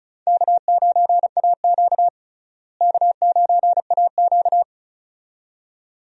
Morse code, heard as a keyed tone, about 36 wpm, 700 Hz. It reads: K9AQ K9AQ
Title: USA Call Signs - 4 Characters - Repeated 2x - 35wpm